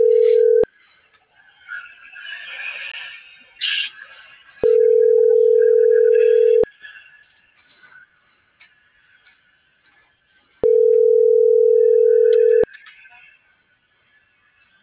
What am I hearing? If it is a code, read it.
Ringback tone (North America), 440 + 480 Hz.